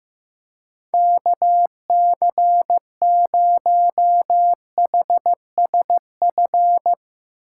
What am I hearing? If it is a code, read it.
Morse code, heard as a keyed tone, about 16 wpm, 700 Hz. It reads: KC0HSF
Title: Speed Racing - USA Call Signs 15wpm